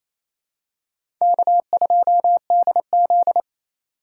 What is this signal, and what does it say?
Morse code, heard as a keyed tone, about 28 wpm, 700 Hz. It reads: K2DZ
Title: USA Call Signs - 4x Character Spacing - 28wpm